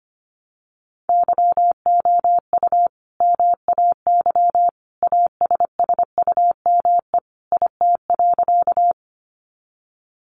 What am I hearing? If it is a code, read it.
Morse code, heard as a keyed tone, about 25 wpm, 700 Hz. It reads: YOU MAY ASSUME IT.
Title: Sentences from Top 1500 Words 25wpm